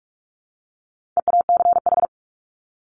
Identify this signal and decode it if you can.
Morse code, heard as a keyed tone, about 44 wpm, 700 Hz. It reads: EACH
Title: Top 200 Words 45wpm